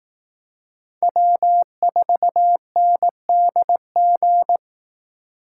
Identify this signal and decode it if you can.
Morse code, heard as a keyed tone, about 18 wpm, 700 Hz. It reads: W4NDG
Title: Speed Racing - USA 1x3 Call Signs 15wpm